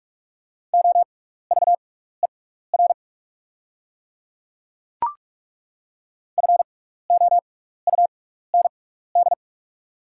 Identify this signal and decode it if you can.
Morse code, heard as a keyed tone, about 43 wpm, spaced as for 15 wpm, 690 Hz. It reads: OVER FOUND
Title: 6x Character Spacing - Top 500 Words - 45wpm